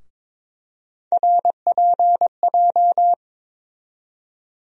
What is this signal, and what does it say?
Morse code, heard as a keyed tone, about 21 wpm, 700 Hz. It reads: RPJ
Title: USA Call Sign Postfixes - 3 Letters 22wpm